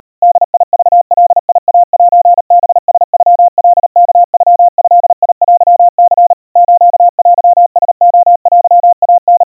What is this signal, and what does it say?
Morse code (Japanese Wabun code), heard as a keyed tone, about 37 wpm, 700 Hz. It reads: ボクガイセハラノカワノドテニ ステラレテイタ
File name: kawa / Morse sample / kamyu 140 cpm 700 Hz Crispy